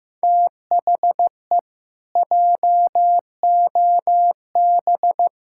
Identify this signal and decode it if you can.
Morse code, heard as a keyed tone, about 15 wpm, 700 Hz. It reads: THE JOB
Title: Pangram Challenge v2 - 15wpm